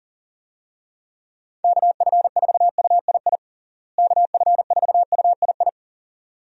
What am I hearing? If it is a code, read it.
Morse code, heard as a keyed tone, about 37 wpm, 700 Hz. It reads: KF4UII KF4UII
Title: USA Call Signs - 6 Characters - Repeated 2x - 40wpm